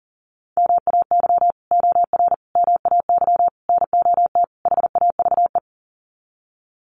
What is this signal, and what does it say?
Morse code, heard as a keyed tone, about 40 wpm, 700 Hz. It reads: MAY OR MAY NOT HAVE
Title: Sets of 5 Words Encore2 40wpm